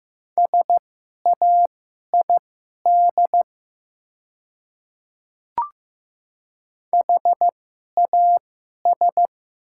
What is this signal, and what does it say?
Morse code, heard as a keyed tone, about 15 wpm, spaced as for 11 wpm, 700 Hz. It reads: SAID HAS
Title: 2x Character Spacing - Top 200 Words - 15wpm